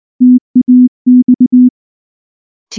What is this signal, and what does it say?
Morse code, heard as a keyed tone, about 19 wpm, 250 Hz. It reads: TAX
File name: morse / morse-training-20WPM-2